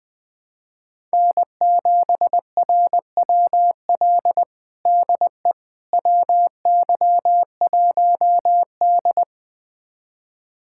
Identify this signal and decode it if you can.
Morse code, heard as a keyed tone, about 20 wpm, 700 Hz. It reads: N7RWL DE WY1D
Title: USA Callsign DE Callsign 20wpm